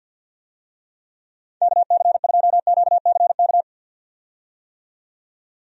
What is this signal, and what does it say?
Morse code, heard as a keyed tone, about 51 wpm, 700 Hz. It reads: KC2QCK
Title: USA Call Signs - 6 Characters - 50wpm